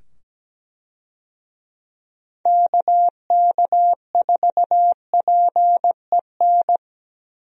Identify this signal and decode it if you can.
Morse code, heard as a keyed tone, about 17 wpm, 700 Hz. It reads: KK4PEN
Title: USA Call Signs - 6 Characters - 17wpm